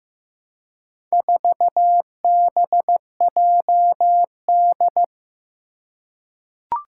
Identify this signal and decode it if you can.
Morse code, heard as a keyed tone, about 15 wpm, 700 Hz. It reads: 4BJD
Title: Four Challenging Letter-Number Combinations 15wpm